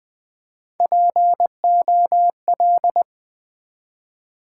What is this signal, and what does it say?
Morse code, heard as a keyed tone, about 20 wpm, 700 Hz. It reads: POL
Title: Build A Word - 3 Letters At A Time - 6-Letter Words - 20wpm